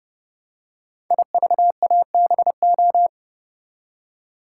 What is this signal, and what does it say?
Morse code, heard as a keyed tone, about 29 wpm, 700 Hz. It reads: IVABO
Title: Code Groups - 5 Letters - 30wpm